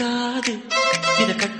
There is no code.